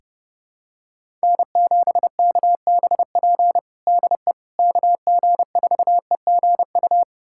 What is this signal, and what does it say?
Morse code, heard as a keyed tone, about 30 wpm, 700 Hz. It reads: N7KBP DE KG4EGU
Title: USA Callsign DE Callsign 30wpm